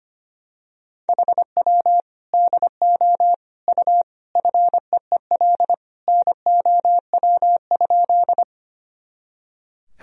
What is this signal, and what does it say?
Morse code, heard as a keyed tone, about 25 wpm, 700 Hz. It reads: HW DO U FEEL NOW?
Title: Sentences from Top 400 Words and Abbreviations 25wpm